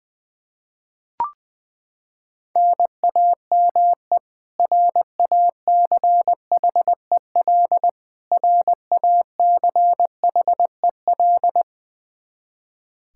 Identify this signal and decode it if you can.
Morse code, heard as a keyed tone, about 20 wpm, 700 Hz. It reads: NAME RACHEL RACHEL
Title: Practice QSO - Exchange 1 - Beginning - 20wpm